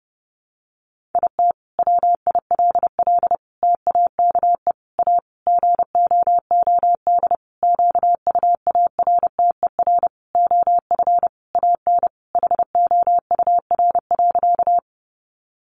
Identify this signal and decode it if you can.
Morse code, heard as a keyed tone, about 30 wpm, 700 Hz. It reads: IT WILL TAKE A GOOD QUARTER OF AN HOUR.
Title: Sentences from Top 1300 Words 30wpm